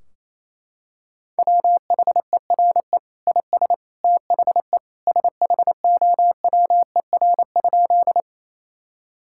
Morse code, heard as a keyed tone, about 28 wpm, 700 Hz. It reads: WHERE IS THE SHOWER?
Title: Sentences from Top 1600 Words 28wpm